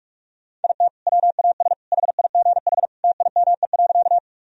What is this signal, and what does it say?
Morse code, heard as a keyed tone, about 46 wpm, 700 Hz. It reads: IT WAS HIGH TIME.
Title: Sentences from Top 1000 Words - Review Part 1 - 45wpm